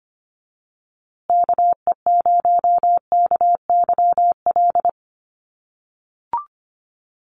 Morse code, heard as a keyed tone, about 25 wpm, 700 Hz. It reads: KE0KYL
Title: USA Call Signs Repeated 3x 25wpm